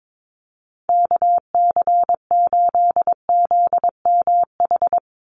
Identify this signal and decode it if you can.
Morse code, heard as a keyed tone, about 22 wpm, 700 Hz. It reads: KC8ZMH
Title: USA Call Signs 22wpm